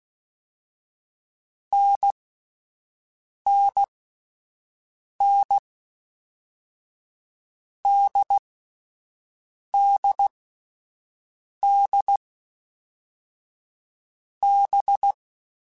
Morse code, heard as a keyed tone, about 16 wpm, 800 Hz. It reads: N N N D D D B
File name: Short Runs / Short Runs-mp3s / NDB NDB6 @10